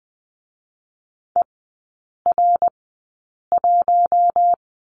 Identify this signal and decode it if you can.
Morse code, heard as a keyed tone, about 20 wpm, spaced as for 8 wpm, 700 Hz. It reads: ER1
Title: 067 Lesson 12e - Letter d - Three Characters 20wpm